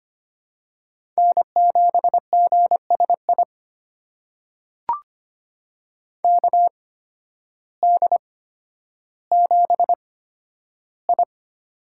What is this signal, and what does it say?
Morse code, heard as a keyed tone, about 25 wpm, spaced as for 12 wpm, 700 Hz. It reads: N7GSI KD7I
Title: USA Call Signs - 5 Characters - 8x Character Spacing - 25wpm